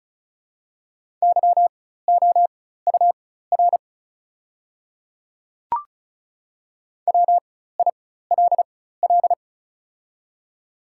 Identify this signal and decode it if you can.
Morse code, heard as a keyed tone, about 35 wpm, spaced as for 16 wpm, 700 Hz. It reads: YOUR WILL
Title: 4x Character Spacing - Top 100 Words - 35wpm